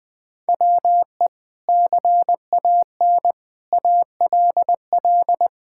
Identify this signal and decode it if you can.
Morse code, heard as a keyed tone, about 20 wpm, 700 Hz. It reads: WE CAN ALL
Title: Sentences with 6-Word Phrases - Spoken Beginning - 20wpm